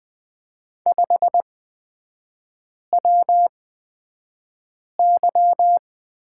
Morse code, heard as a keyed tone, about 20 wpm, spaced as for 5 wpm, 700 Hz. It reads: 5WY
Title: 224 Lesson 36e - BT - Three Characters 20wpm